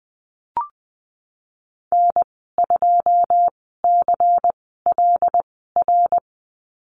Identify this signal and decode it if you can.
Morse code, heard as a keyed tone, about 20 wpm, spaced as for 15 wpm, 700 Hz. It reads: N2CLR
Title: 128 Lesson 21g - Number 6 - Call Signs 20wpm